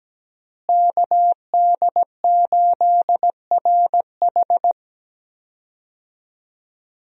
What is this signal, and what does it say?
Morse code, heard as a keyed tone, about 17 wpm, 700 Hz. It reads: KD8RH
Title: USA Call Signs - 5 Characters - Repeated 2x - 17wpm